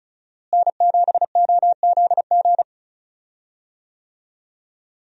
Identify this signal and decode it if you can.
Morse code, heard as a keyed tone, about 34 wpm, 700 Hz. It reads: N7OZG
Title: USA Call Signs - 5 Characters - Repeated 3x - 35wpm